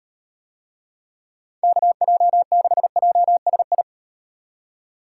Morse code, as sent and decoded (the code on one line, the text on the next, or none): -.- .--- -.... .--- ... ..
KJ6JSI